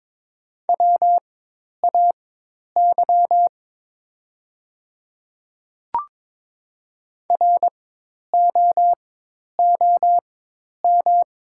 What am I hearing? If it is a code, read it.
Morse code, heard as a keyed tone, about 22 wpm, spaced as for 10 wpm, 700 Hz. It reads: WAY ROOM
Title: 4x Character Spacing - Top 500 Words - 22wpm